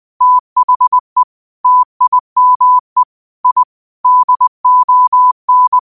Morse code, heard as a keyed tone, about 20 wpm, 1000 Hz. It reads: THE TIME I DON